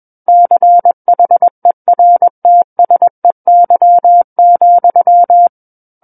Morse code, heard as a keyed tone, about 21 wpm, 700 Hz. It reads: CHERTSEY,
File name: Book One / 12-Chapter 12 - 21 WPM